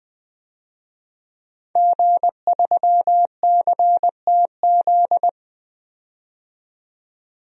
Morse code, heard as a keyed tone, about 20 wpm, 700 Hz. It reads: G3CTZ
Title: DXCC-100 Call Signs - 5 Characters - 20wpm